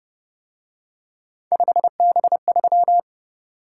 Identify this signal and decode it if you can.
Morse code, heard as a keyed tone, about 29 wpm, 700 Hz. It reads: 5B3